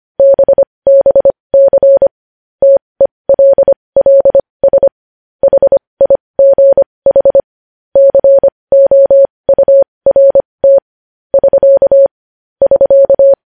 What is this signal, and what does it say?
Morse code, heard as a keyed tone, about 25 wpm, spaced as for 20 wpm, 550 Hz. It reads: BBC TELLS HIGH COURT <SK> <SK>
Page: news-headlines-in-morse-code-at-20-wpm